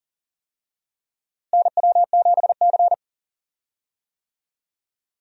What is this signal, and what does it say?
Morse code, heard as a keyed tone, about 40 wpm, 700 Hz. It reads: NW7C
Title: USA Call Signs - 4 Characters - Repeated 3x - 40wpm